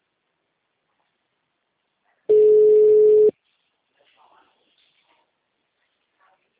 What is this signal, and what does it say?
Ringback tone (Europe, 425 Hz).